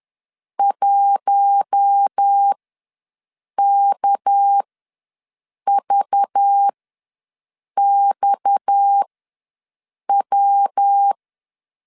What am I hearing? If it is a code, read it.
Morse code, heard as a keyed tone, about 10 wpm, spaced as for 6 wpm, 800 Hz. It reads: 1KVXW
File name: LEZIONI / 7 GF9Z / 7 GF ws 10wpm 3918